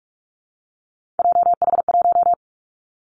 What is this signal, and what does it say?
Morse code, heard as a keyed tone, about 44 wpm, 700 Hz. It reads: JH1
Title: Callsign Prefixes for Top 25 DXCC - 45wpm